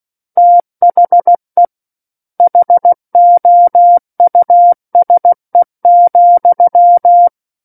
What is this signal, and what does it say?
Morse code, heard as a keyed tone, about 16 wpm, 700 Hz. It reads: THE HOUSE,
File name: 03-Chapter-03 16 WPM